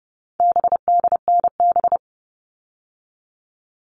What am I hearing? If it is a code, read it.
Morse code, heard as a keyed tone, about 30 wpm, 700 Hz. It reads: BDNB
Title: All Cut Numbers - 4-Digit 30wpm